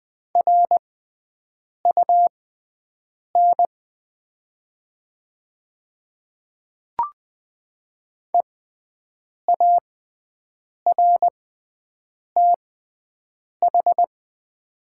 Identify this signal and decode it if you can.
Morse code, heard as a keyed tone, about 20 wpm, spaced as for 7 wpm, 700 Hz. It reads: RUN EARTH
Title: 6x Character Spacing - Top 300 Words - 20wpm